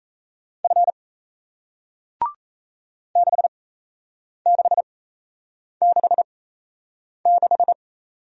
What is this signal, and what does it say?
Morse code, heard as a keyed tone, about 37 wpm, spaced as for 8 wpm, 700 Hz. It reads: F 6666